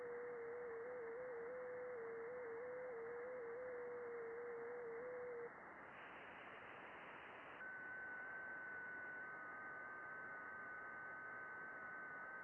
FT8.
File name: contests / june-vhf-k5tr- -0200